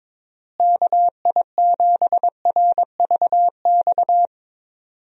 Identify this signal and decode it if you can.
Morse code, heard as a keyed tone, about 22 wpm, 700 Hz. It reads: KI7RVX
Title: USA Call Signs - 6 Characters - Repeated 2x - 22wpm